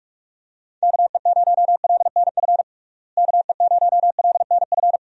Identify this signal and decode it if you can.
Morse code, heard as a keyed tone, about 46 wpm, 700 Hz. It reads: KE0LNF KE0LNF
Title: USA Call Signs - 6 Characters - Repeated 3x - 45wpm